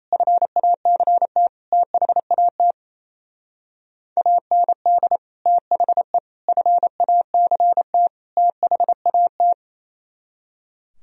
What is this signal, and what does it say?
Morse code, heard as a keyed tone, about 35 wpm, 700 Hz. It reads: FACT THAT AND THE FACT THAT
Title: Speed-Racing - Sets of 4 Words 28wpm